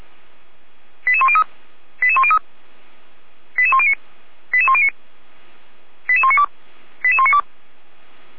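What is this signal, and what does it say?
Five-tone selective call (ZVEI), heard as quick runs of five tones, tones of about 70 ms. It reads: 88103 88103 88109 88109 88192 88192
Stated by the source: ZVEI five-tone call 88265 received by SDR, from audio.f85360.feuerwehr-orlamuende.org fire brigade (Feuerwehr Orlamünde) archive.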